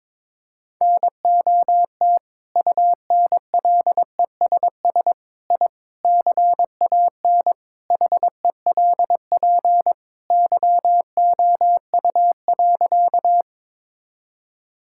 Morse code, heard as a keyed tone, about 22 wpm, 700 Hz. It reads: NOT UNLESS I CAN HELP YOU.
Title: Sentences from Top 1000 Words - Review Part 6 - 22wpm